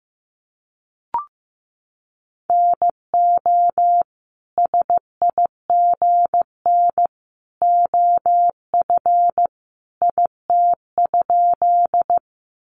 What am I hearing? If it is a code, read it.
Morse code, heard as a keyed tone, about 15 wpm, 700 Hz. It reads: NO SIGN OF IT?
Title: Sentences from Top 1000 Words 15wpm